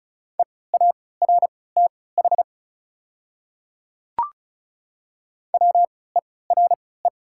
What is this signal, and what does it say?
Morse code, heard as a keyed tone, about 36 wpm, spaced as for 20 wpm, 700 Hz. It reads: EARTH WERE
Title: 3x Character Spacing - Top 200 Words - 35wpm